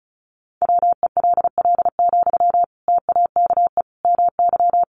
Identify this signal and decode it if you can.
Morse code, heard as a keyed tone, about 34 wpm, 700 Hz. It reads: WELL, TAKE MY
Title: Sentences from Top 1100 Words 35wpm